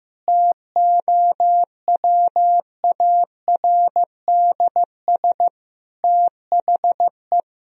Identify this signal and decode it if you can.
Morse code, heard as a keyed tone, about 14 wpm, 700 Hz. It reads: TOWARDS THE
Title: Sentences from Top 1000 Words - Review Part 2 - 15wpm